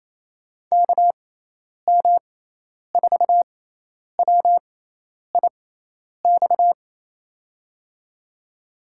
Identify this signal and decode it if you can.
Morse code, heard as a keyed tone, about 28 wpm, spaced as for 10 wpm, 700 Hz. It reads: KM4WIX
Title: USA Call Signs - 6x Character Spacing - 28wpm